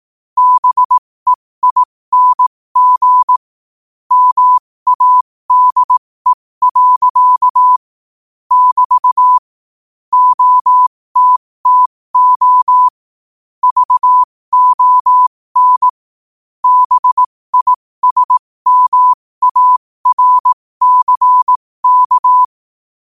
Morse code, heard as a keyed tone, about 18 wpm, spaced as for 15 wpm, 1000 Hz. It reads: BEING MADE. <BT> OTTO VON BISMARCK
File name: qotdfiles / 15wpm